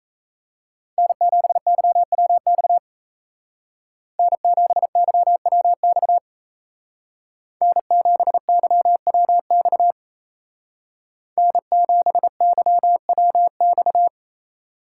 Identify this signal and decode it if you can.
Morse code, heard as a keyed tone, about 36 wpm, 700 Hz. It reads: N7YWX N7YWX N7YWX N7YWX